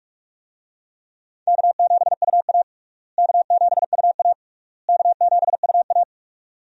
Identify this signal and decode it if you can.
Morse code, heard as a keyed tone, about 46 wpm, 700 Hz. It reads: K7UA K7UA K7UA
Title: USA Call Signs - 4 Characters - Repeated 3x - 45wpm